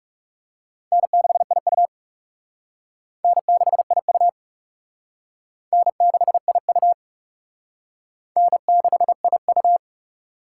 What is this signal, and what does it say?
Morse code, heard as a keyed tone, about 35 wpm, 700 Hz. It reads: N6IU N6IU N6IU N6IU
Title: Speed Racing - USA 1x2 Call Signs 30wpm